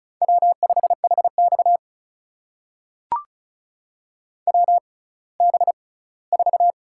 Morse code, heard as a keyed tone, about 35 wpm, spaced as for 20 wpm, 700 Hz. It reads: W5HX WB4
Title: USA Call Signs - 6x Character Spacing - 35wpm